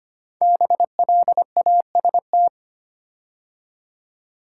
Morse code, heard as a keyed tone, about 25 wpm, 700 Hz. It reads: BLAST
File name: 5 Letter Words - First 3 Letters Given - 25wpm